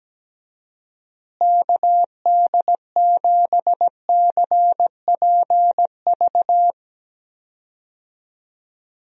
Morse code, heard as a keyed tone, about 17 wpm, 700 Hz. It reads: KD7CPV
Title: USA Call Signs - 6 Characters - Repeated 2x - 17wpm